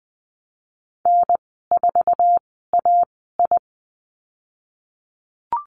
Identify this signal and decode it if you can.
Morse code, heard as a keyed tone, about 20 wpm, 700 Hz. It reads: N 4 A I
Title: 048 Lesson 9g - Number 4 - Call Signs 20wpm